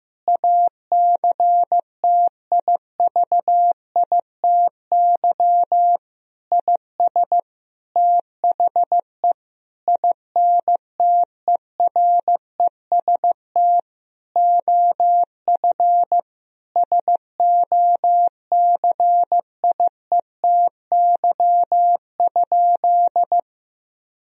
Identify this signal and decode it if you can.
Morse code, heard as a keyed tone, about 15 wpm, 700 Hz. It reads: ACTIVITY IS THE INTEREST OF SOCIETY?
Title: Sentences from Top 1000 Words - Review Part 4 - 15wpm